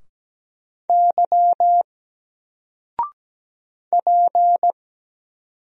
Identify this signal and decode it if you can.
Morse code, heard as a keyed tone, about 17 wpm, 700 Hz. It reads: Y P
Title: Single Letter-Number 17wpm